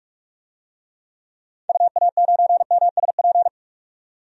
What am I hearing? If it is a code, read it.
Morse code, heard as a keyed tone, about 46 wpm, 700 Hz. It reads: UA9MSP
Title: DXCC-100 Call Signs - 6 Characters - 45wpm